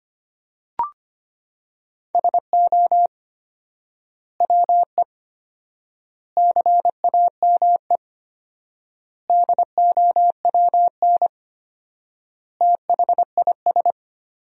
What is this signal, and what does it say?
Morse code, heard as a keyed tone, about 25 wpm, 700 Hz. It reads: SO WE CAME DOWN THIS
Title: 4x Word-Spacing - Sentences from Top 400 Words - 25wpm